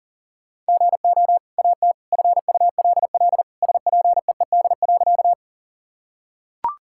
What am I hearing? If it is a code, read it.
Morse code, heard as a keyed tone, about 40 wpm, 700 Hz. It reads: GO AT FULL SPEED.
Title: Sentences from Top 1000 Words - Review Part 5 - 40wpm